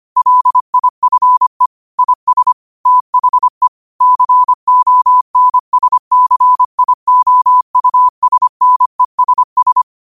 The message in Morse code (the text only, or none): LIFE IS THE CONSCIOUSNESS